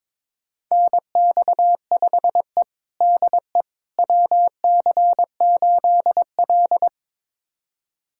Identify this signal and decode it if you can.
Morse code, heard as a keyed tone, about 22 wpm, 700 Hz. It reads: NX5E DE WC8L
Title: USA Callsign DE Callsign - 4 Characters - 22wpm